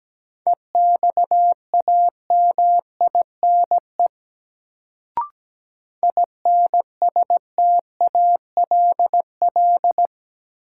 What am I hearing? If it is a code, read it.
Morse code, heard as a keyed tone, about 17 wpm, 700 Hz. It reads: EXAMINE INSTALL
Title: Sentences from Top 1600 Words - New Word Drill - 17wpm